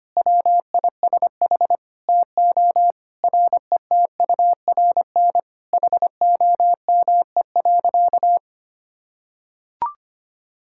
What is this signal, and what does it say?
Morse code, heard as a keyed tone, about 25 wpm, 700 Hz. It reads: WISH TO RETURN HOME.
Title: Sentences from Top 1600 Words 25wpm